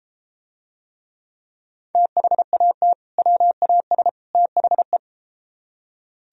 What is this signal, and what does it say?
Morse code, heard as a keyed tone, about 32 wpm, 700 Hz. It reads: THAT WAS THE